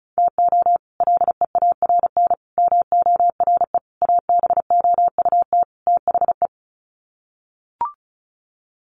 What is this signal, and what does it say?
Morse code, heard as a keyed tone, about 34 wpm, 700 Hz. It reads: TO LEARN MORE ABOUT THE
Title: Sets of 5 Words Encore2 35wpm